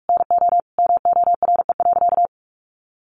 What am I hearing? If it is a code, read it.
Morse code, heard as a keyed tone, about 46 wpm, 700 Hz. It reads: NO MORE.
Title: Sentences from Top 1400 Words 45wpm